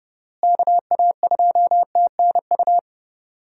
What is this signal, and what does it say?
Morse code, heard as a keyed tone, about 30 wpm, 700 Hz. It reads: KA2TNU
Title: USA Call Signs - 6 Characters - Repeated 2x - 30wpm